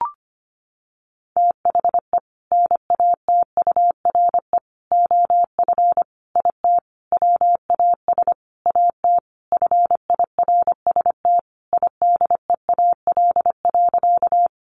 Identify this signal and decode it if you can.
Morse code, heard as a keyed tone, about 25 wpm, 700 Hz. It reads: THE NATURE OF IT WAS AT FIRST IDEAL.
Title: Sentences from Top 1400 Words 25wpm